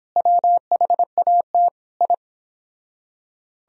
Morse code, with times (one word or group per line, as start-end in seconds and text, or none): 0.2-1.7 WHAT
2.0-2.1 I